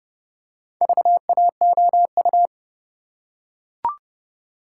Morse code, heard as a keyed tone, about 28 wpm, 700 Hz. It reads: VAOU